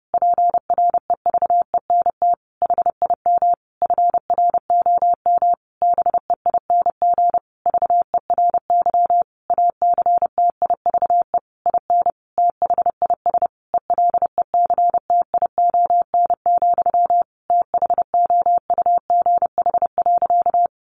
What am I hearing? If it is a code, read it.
Morse code, heard as a keyed tone, about 30 wpm, 700 Hz. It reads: PREVENT HIM FROM BEING VERY ACTIVE IN THIS ELECTION, THOUGH.
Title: Sentences from Top 1500 Words 30wpm